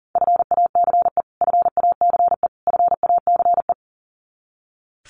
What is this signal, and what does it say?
Morse code, heard as a keyed tone, about 40 wpm, 700 Hz. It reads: FACE FACE FACE
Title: Top 300 Words Repeated 40wpm